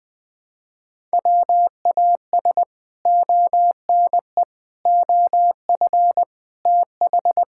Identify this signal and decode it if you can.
Morse code, heard as a keyed tone, about 21 wpm, 700 Hz. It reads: WAS ONE OF TH
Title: Build A Phrase - 5-Word Phrases - 20wpm